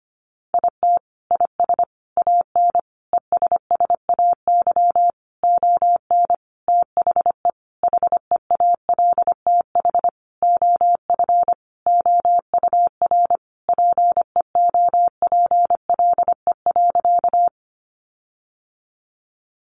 Morse code, heard as a keyed tone, about 25 wpm, 700 Hz. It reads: IT IS AN ESSAY ON THE HEALTH OF OUR PEOPLE.
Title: Sentences from Top 1500 Words 25wpm